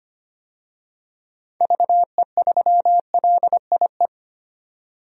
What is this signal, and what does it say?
Morse code, heard as a keyed tone, about 25 wpm, 700 Hz. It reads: VE3LIE
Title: DXCC-100 Call Signs - 6 Characters - 25wpm